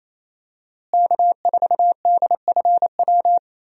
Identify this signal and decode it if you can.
Morse code, heard as a keyed tone, about 28 wpm, 700 Hz. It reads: K4DFW
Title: USA Call Signs - 5 Characters - Repeated 2x - 28wpm